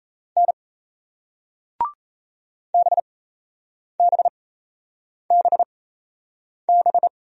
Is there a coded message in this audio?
Yes: Morse code from 0.4 to 7.1 s, heard as a keyed tone, about 38 wpm, spaced as for 8 wpm, 700 Hz. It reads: N BBBB